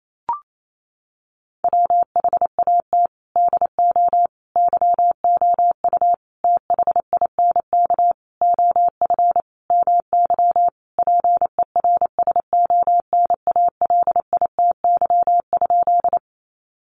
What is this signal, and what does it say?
Morse code, heard as a keyed tone, about 28 wpm, 700 Hz. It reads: WHAT DO YOU THINK OF MY PERSONALITY?
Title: Sentences from Top 1400 Words 28wpm